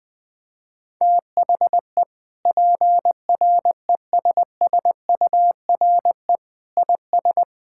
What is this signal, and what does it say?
Morse code, heard as a keyed tone, about 21 wpm, 700 Hz. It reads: THE PRESSURE IS
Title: Sentences from Top 1000 Words - Review Part 6 - 20wpm